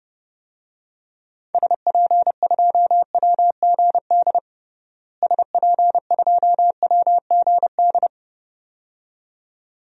Morse code, heard as a keyed tone, about 30 wpm, 700 Hz. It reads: SP2WGD SP2WGD
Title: DXCC-100 Call Signs - 6 Characters - Repeated 2x - 30wpm